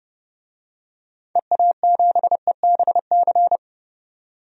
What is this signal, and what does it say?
Morse code, heard as a keyed tone, about 30 wpm, 700 Hz. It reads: EA7EBC
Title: DXCC-100 Call Signs - 6 Characters - 30wpm